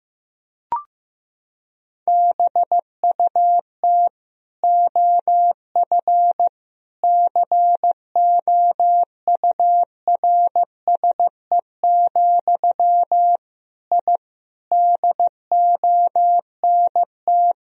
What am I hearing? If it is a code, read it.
Morse code, heard as a keyed tone, about 15 wpm, 700 Hz. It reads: BUT OF COURSE, I DONT
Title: Sentences from Top 1000 Words - Review Part 2 - 15wpm